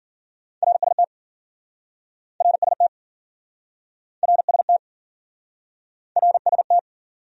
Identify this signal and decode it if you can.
Morse code, heard as a keyed tone, about 53 wpm, 700 Hz. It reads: RST RST RST RST